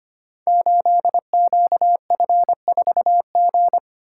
Morse code, heard as a keyed tone, about 25 wpm, 700 Hz. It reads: 8QF4G
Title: Code Groups - 5 Letter:Numbers - 25wpm